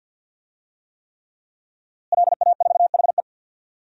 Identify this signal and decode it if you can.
Morse code, heard as a keyed tone, about 52 wpm, 700 Hz. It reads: LA4HE
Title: DXCC-100 Call Signs - 5 Characters - Repeated 3x - 50wpm